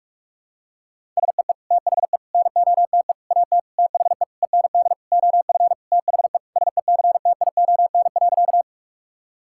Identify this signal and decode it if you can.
Morse code, heard as a keyed tone, about 46 wpm, 700 Hz. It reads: SEE THE NOTE AT THE END OF THE SECTION.
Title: Sentences from Top 1000 Words - Review Part 5 - 45wpm